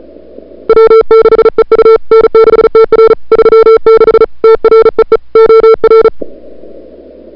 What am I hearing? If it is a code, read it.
Morse code, heard as a keyed tone, about 35 wpm, 440 Hz. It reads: W6EU N6TR 36 TREE OR
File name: CWSprint / February 2007 CW Sprint / 01-2007 CW Sprint Hour1